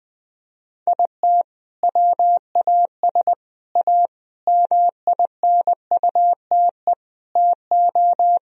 Morse code, heard as a keyed tone, about 20 wpm, 700 Hz. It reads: IT WAS A MINUTE TO
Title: Sentences from Top 1000 Words - Review Part 1 - 20wpm